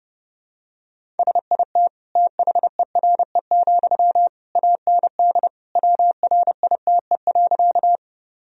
Morse code, heard as a keyed tone, about 32 wpm, 700 Hz. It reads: SIT THERE, AND WRITE.
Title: Sentences from Top 1000 Words - Review Part 1 - 30wpm